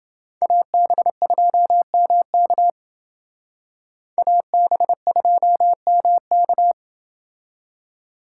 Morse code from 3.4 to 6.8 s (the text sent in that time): AB2MK